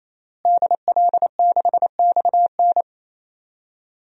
Morse code, heard as a keyed tone, about 28 wpm, 700 Hz. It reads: DL6XN
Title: DXCC-100 Call Signs - 5 Characters - Repeated 3x - 28wpm